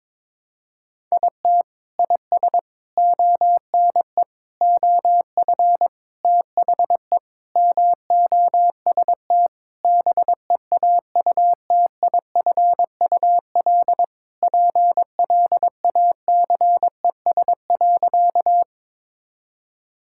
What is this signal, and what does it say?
Morse code, heard as a keyed tone, about 21 wpm, 700 Hz. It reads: IT IS ONE OF THE MOST BEAUTIFUL PLACES.
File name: practice / 22wpm / Sentences with 3-Word Phrases - 22wpm